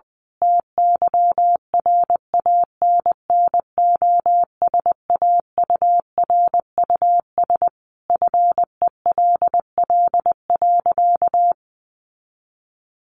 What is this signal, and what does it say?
Morse code, heard as a keyed tone, about 20 wpm, 700 Hz. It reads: TYRANNOSAURUS FELL.
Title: A Sound Of Thunder 20wpm